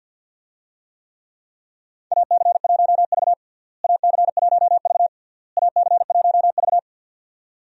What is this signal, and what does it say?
Morse code, heard as a keyed tone, about 47 wpm, 700 Hz. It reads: AC1V AC1V AC1V
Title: USA Call Signs - 4 Characters - Repeated 3x - 50wpm